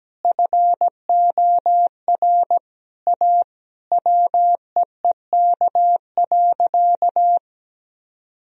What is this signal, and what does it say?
Morse code, heard as a keyed tone, about 17 wpm, 700 Hz. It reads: FOR A WEEK.
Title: Sentences from Top 1000 Words - Review Part 2 - 17wpm